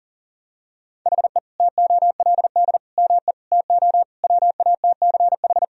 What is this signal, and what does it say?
Morse code, heard as a keyed tone, about 40 wpm, 700 Hz. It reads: HE TOLD ME TO WATCH
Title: Sentences from Top 1000 Words - Review Part 1 - 40wpm